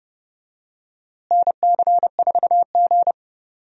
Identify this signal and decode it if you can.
Morse code, heard as a keyed tone, about 30 wpm, 700 Hz. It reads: NC4G
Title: USA Call Signs - 4 Characters - Repeated 2x - 30wpm